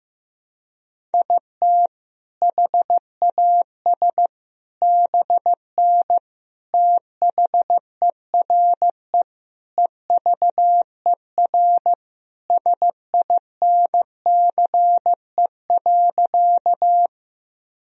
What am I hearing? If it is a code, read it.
Morse code, heard as a keyed tone, about 15 wpm, 700 Hz. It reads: IT HAS BN THERE EVER SINCE.
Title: Sentences from Top 400 Words and Abbreviations 15wpm